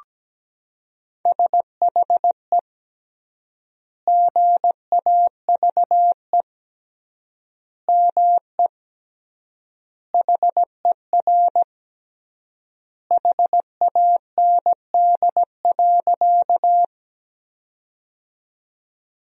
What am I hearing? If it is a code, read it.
Morse code, heard as a keyed tone, about 17 wpm, 700 Hz. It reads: SHE GAVE ME HER HAND.
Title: 3x Word-Spacing - Sentences from Top 500 Words - 17wpm